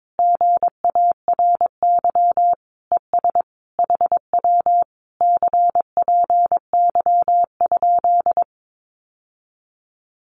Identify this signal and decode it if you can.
Morse code, heard as a keyed tone, about 22 wpm, 700 Hz. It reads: GARY ES HW CPY?
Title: Common QSO Phrases Encore 22wpm